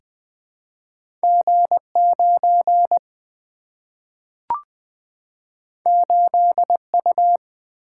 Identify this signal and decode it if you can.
Morse code, heard as a keyed tone, about 20 wpm, 700 Hz. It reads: G9 8U
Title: Two Letter-Number Combinations - 20wpm